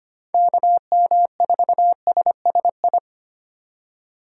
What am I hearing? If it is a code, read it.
Morse code, heard as a keyed tone, about 25 wpm, 700 Hz. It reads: KM4SSI
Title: USA Call Signs - 6 Characters - 3x Character Spacing - 25wpm